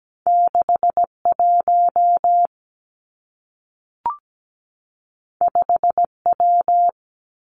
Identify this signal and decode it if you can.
Morse code, heard as a keyed tone, about 17 wpm, 700 Hz. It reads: 6 1 5 W
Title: Two Challenging Letter-Number Combinations 17wpm